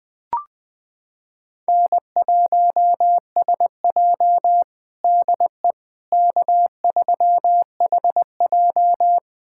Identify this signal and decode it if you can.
Morse code, heard as a keyed tone, about 20 wpm, 700 Hz. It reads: N1SJ DE K3HJ
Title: USA Callsign DE Callsign - 4 Characters - 20wpm